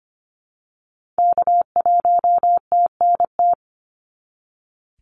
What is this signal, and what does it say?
Morse code, heard as a keyed tone, about 25 wpm, 700 Hz. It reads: K1TNT
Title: NA SOTA Activator Callsigns 25wpm